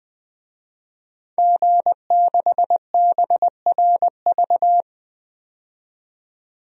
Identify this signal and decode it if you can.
Morse code, heard as a keyed tone, about 20 wpm, 700 Hz. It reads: G6BRV
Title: DXCC-100 Call Signs - 5 Characters - 20wpm